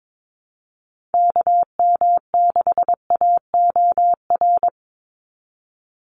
Morse code, heard as a keyed tone, about 22 wpm, 700 Hz. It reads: KM6AOR
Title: USA Call Signs 22wpm